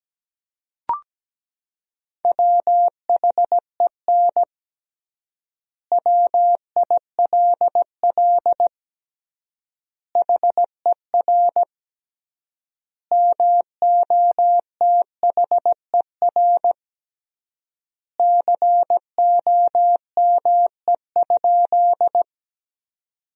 Morse code, heard as a keyed tone, about 17 wpm, 700 Hz. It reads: WHEN WILL HER MOTHER COME?
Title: 3x Word-Spacing - Sentences from Top 200 Words - 17wpm